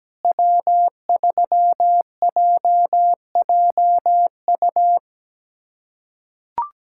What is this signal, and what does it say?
Morse code, heard as a keyed tone, about 17 wpm, 700 Hz. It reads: W3JJU
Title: USA Callsign DE Callsign 17wpm